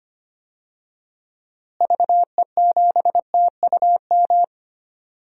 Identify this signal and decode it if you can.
Morse code, heard as a keyed tone, about 25 wpm, 700 Hz. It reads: VE7TUM
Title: DXCC-100 Call Signs - 6 Characters - Repeated 2x - 25wpm